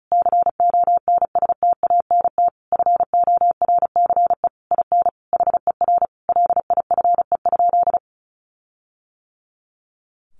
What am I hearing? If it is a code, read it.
Morse code, heard as a keyed tone, about 34 wpm, 700 Hz. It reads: CONSTANT FORCE IN HER LIFE?
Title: Sentences from Top 1500 Words 35wpm